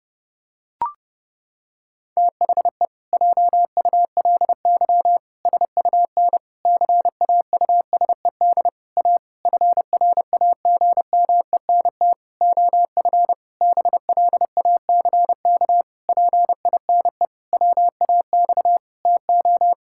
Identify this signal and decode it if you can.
Morse code, heard as a keyed tone, about 32 wpm, 700 Hz. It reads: THE JULY SUN CAUSED A FRAGMENT OF BLACK PINE WAX TO
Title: Pangram Challenge v2 - 30wpm